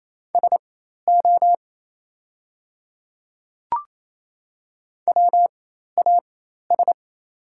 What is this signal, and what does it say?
Morse code, heard as a keyed tone, about 28 wpm, spaced as for 13 wpm, 700 Hz. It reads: SO WAS
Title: 4x Character Spacing - Top 100 Words - 28wpm